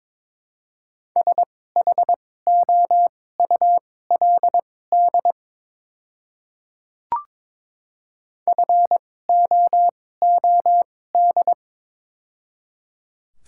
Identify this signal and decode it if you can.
Morse code, heard as a keyed tone, about 22 wpm, spaced as for 16 wpm, 700 Hz. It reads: SHOULD FOOD
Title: 2x Character Spacing - Top 300 Words - 22wpm